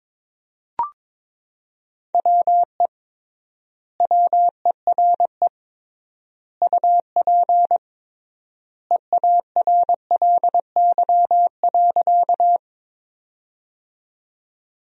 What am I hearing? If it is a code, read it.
Morse code, heard as a keyed tone, about 22 wpm, 700 Hz. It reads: WE WERE UP EARLY.
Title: 3x Word-Spacing - Sentences from Top 400 Words - 22wpm